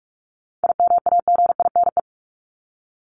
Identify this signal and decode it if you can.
Morse code, heard as a keyed tone, about 44 wpm, 700 Hz. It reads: IMAGINE
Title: Top 1251-1500 Words 45wpm